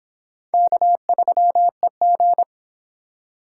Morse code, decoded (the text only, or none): K3EG